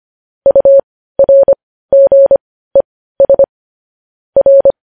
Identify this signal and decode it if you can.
Morse code, heard as a keyed tone, about 25 wpm, spaced as for 15 wpm, 550 Hz. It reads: URGES R
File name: podcast / News headlines-1x-25x15